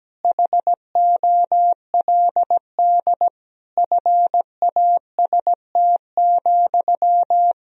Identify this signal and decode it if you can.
Morse code, heard as a keyed tone, about 17 wpm, 700 Hz. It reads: HOLD FAST,
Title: Sentences from Top 1000 Words - Review Part 2 - 17wpm